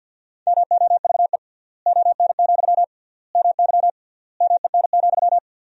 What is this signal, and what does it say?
Morse code, heard as a keyed tone, about 52 wpm, 700 Hz. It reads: MOVE ON, MY MEN,
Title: 3x Word-Spacing - Sentences from Top 500 Words - 50wpm